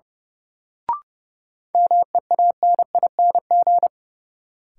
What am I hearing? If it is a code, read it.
Morse code, heard as a keyed tone, about 30 wpm, 700 Hz. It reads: MEANING
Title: Sentences from Top 900 Words - New Word Drill - 30wpm